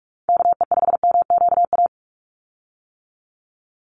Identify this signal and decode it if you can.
Morse code, heard as a keyed tone, about 44 wpm, 700 Hz. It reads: KE5MQA
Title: USA Call Signs 45wpm